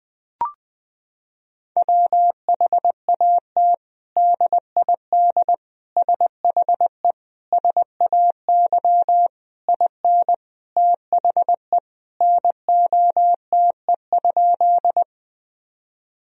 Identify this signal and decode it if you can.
Morse code, heard as a keyed tone, about 20 wpm, 700 Hz. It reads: WHAT DID SHE SAY IN THE NOTE?
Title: Sentences from Top 1000 Words - Review Part 2 - 20wpm